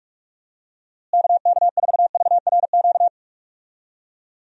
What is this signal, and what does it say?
Morse code, heard as a keyed tone, about 42 wpm, 700 Hz. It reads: KK4VRQ
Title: USA Call Signs - 6 Characters - Repeated 3x - 45wpm